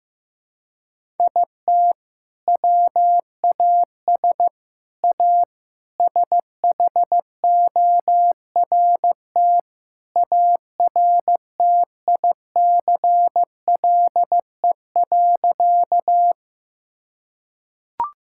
Morse code, heard as a keyed tone, about 15 wpm, 700 Hz. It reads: IT WAS A SHORT ARTICLE.
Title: Sentences from Top 1000 Words - Review Part 5 - 15wpm